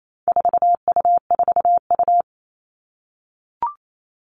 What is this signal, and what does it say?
Morse code, heard as a keyed tone, about 28 wpm, 700 Hz. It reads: 4U4U
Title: All Cut Numbers - 4-Digit 28wpm